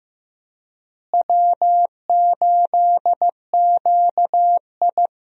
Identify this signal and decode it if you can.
Morse code, heard as a keyed tone, about 15 wpm, 700 Hz. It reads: W8QI
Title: USA Call Signs - 4 Characters - Repeated 3x - 15wpm